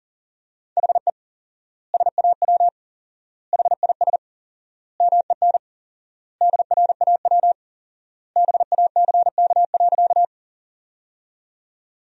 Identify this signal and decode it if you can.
Morse code, heard as a keyed tone, about 40 wpm, 700 Hz. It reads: HE SAW HIS MEN DRAW BACK.
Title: 4x Word-Spacing - Sentences from Top 300 Words - 40wpm